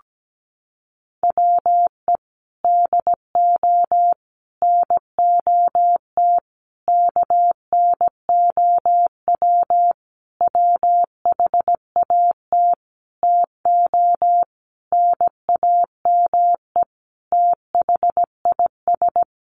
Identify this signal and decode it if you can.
Morse code, heard as a keyed tone, about 17 wpm, 700 Hz. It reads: WE DO NOT KNOW WHAT TO NAME THIS
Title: Sentences from Top 1500 Words 17wpm